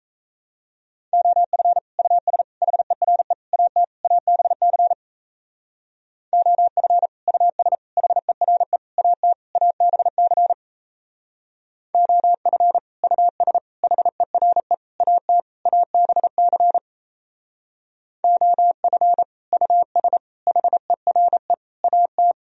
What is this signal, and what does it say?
Morse code, heard as a keyed tone, about 38 wpm, 700 Hz. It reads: OF US HERE AT ABC OF US HERE AT ABC OF US HERE AT ABC OF US HERE AT